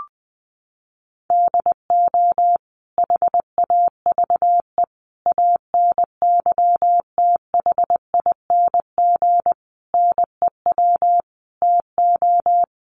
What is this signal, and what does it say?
Morse code, heard as a keyed tone, about 20 wpm, 700 Hz. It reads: DO HAVE ANYTHING NEW TO
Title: Sentences from Top 1300 Words 20wpm